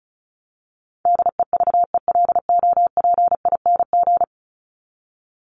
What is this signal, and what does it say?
Morse code, heard as a keyed tone, about 34 wpm, 700 Hz. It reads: DEVELOPING
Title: ING Suffix - Top 100 Words with Base Words 35wpm